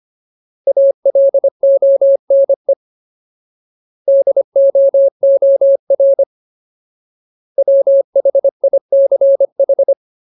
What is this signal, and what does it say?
Morse code, heard as a keyed tone, about 25 wpm, 550 Hz. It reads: ALONE DOOR WHICH
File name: Ultimate Word List / 25WPM Ultimate